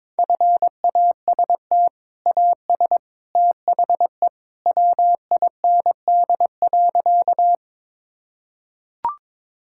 Morse code, heard as a keyed tone, about 22 wpm, 700 Hz. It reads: FAST AS THE WIND.
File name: practice / 22wpm / Sentences with 2-Word Phrases - 22wpm